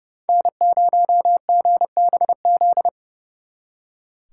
Morse code, heard as a keyed tone, about 36 wpm, 700 Hz. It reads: N0GBZ
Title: Speed Racing - USA 1x3 Call Signs 30wpm